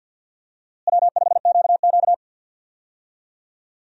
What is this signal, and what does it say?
Morse code, heard as a keyed tone, about 50 wpm, 700 Hz. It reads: W5QQ
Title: USA Call Signs - 4 Characters - Repeated 3x - 50wpm